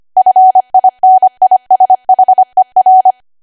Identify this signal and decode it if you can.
Morse code, heard as a keyed tone, about 25 wpm, 750 Hz. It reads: FINISHER